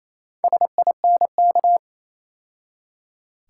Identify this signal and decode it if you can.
Morse code, heard as a keyed tone, about 27 wpm, 700 Hz. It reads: SINK
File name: Easy Words - Speed-Racing with Hints - 22wpm